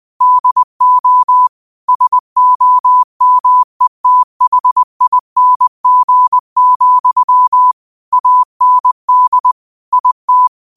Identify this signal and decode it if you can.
Morse code, heard as a keyed tone, about 20 wpm, 1000 Hz. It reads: DO SOMETHING, AND IT
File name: qotdfiles / 20wpm